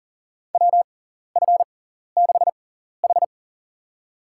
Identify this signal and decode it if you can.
Morse code, heard as a keyed tone, about 40 wpm, spaced as for 14 wpm, 700 Hz. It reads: WF6H
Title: USA Call Signs - 4 Characters - 6x Character Spacing - 40wpm